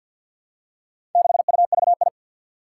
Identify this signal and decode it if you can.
Morse code, heard as a keyed tone, about 52 wpm, 700 Hz. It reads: 6UVI
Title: Four Challenging Letter-Number Combinations 50wpm